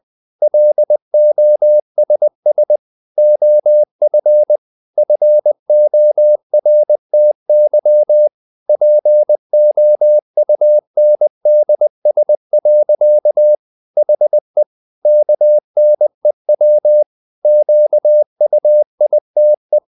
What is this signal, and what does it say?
Morse code, heard as a keyed tone, about 20 wpm, 600 Hz. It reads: LOSS OF FORTY POUNDS. HE KNEW QUITE
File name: the old man and the sea part 2 20wpm